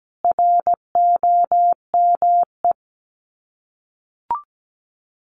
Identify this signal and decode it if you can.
Morse code, heard as a keyed tone, about 17 wpm, 700 Hz. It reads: ROME
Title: Common QSO Phrases 17wpm